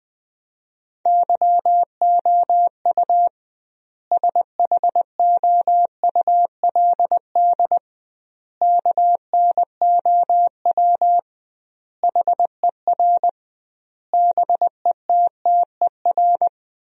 Morse code, heard as a keyed tone, about 20 wpm, 700 Hz. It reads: YOU SHOULD KNOW HER BETTER
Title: 2x Word-Spacing - Sentences from Top 400 Words - 20wpm